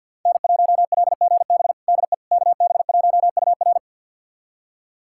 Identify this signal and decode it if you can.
Morse code, heard as a keyed tone, about 52 wpm, 700 Hz. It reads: N1LGB DE KB1UR
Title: USA Callsign DE Callsign - 5 Characters - 50wpm